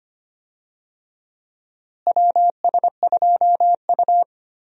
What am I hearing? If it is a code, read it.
Morse code, heard as a keyed tone, about 25 wpm, 700 Hz. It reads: WS2U
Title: USA Call Signs - 4 Characters - Repeated 2x - 25wpm